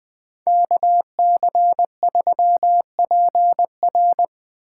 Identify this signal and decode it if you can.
Morse code, heard as a keyed tone, about 20 wpm, 700 Hz. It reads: KC3PR
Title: USA Call Signs - 5 Characters - 2x Character Spacing - 20wpm